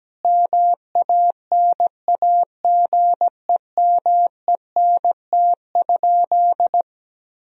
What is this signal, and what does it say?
Morse code, heard as a keyed tone, about 17 wpm, 700 Hz. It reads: MANAGEMENT?
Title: Sentences from Top 1000 Words - Review Part 4 - 17wpm